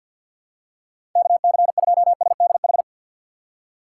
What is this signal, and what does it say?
Morse code, heard as a keyed tone, about 48 wpm, 700 Hz. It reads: KC2SDH
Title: USA Call Signs - 6 Characters - 6x Character Spacing - 50wpm